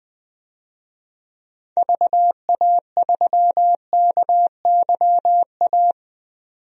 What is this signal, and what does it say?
Morse code, heard as a keyed tone, about 21 wpm, 700 Hz. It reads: VA3KYA
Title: DXCC-100 Call Signs - 6 Characters - 20wpm